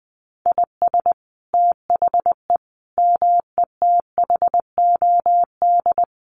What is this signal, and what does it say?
Morse code, heard as a keyed tone, about 20 wpm, 700 Hz. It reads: IS THE METHOD
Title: Sentences from Top 1500 Words 20wpm